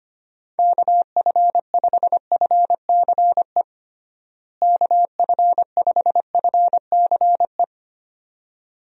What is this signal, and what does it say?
Morse code, heard as a keyed tone, about 25 wpm, 700 Hz. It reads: KF5FCE KF5FCE
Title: USA Call Signs - 6 Characters - Repeated 2x - 25wpm